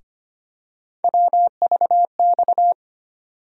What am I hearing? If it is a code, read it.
Morse code, heard as a keyed tone, about 25 wpm, 700 Hz. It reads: WVX
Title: USA Call Sign Postfixes - 3 Letters 25wpm